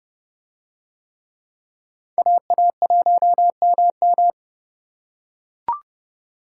Morse code, heard as a keyed tone, about 30 wpm, 700 Hz. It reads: AA1MM